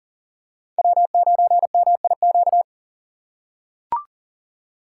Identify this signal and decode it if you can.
Morse code, heard as a keyed tone, about 40 wpm, 700 Hz. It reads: W9MIQ
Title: USA Call Signs - 2x Character Spacing - 40wpm